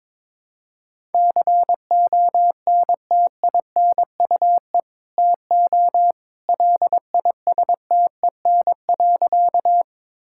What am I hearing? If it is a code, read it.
Morse code, heard as a keyed tone, about 22 wpm, 700 Hz. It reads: CONTINUE TO LISTEN.
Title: Sentences from Top 1000 Words - Review Part 6 - 22wpm